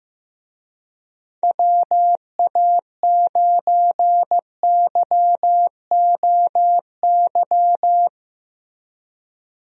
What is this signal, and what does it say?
Morse code, heard as a keyed tone, about 15 wpm, 700 Hz. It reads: WA9YOY
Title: USA Call Signs - 6 Characters - Repeated 2x - 15wpm